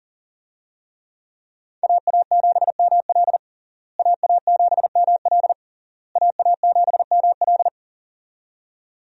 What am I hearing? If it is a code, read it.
Morse code, heard as a keyed tone, about 40 wpm, 700 Hz. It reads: AA7ML AA7ML AA7ML
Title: USA Call Signs - 5 Characters - Repeated 3x - 40wpm